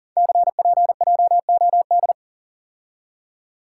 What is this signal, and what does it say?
Morse code, heard as a keyed tone, about 43 wpm, 700 Hz. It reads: CPJOD
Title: Code Groups - 5 Letters - 40wpm